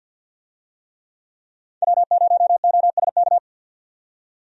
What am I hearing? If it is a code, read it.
Morse code, heard as a keyed tone, about 46 wpm, 700 Hz. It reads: W0OSK
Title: USA Call Signs - 5 Characters - 50wpm